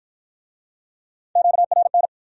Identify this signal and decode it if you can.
Morse code, heard as a keyed tone, about 43 wpm, 700 Hz. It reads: QRN
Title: Speed-Racing Q-Codes - 35wpm